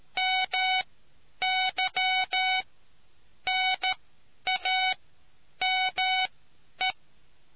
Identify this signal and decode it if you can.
Morse code, heard as a keyed tone, about 13 wpm, spaced as for 9 wpm, 750 Hz. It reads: MY NAME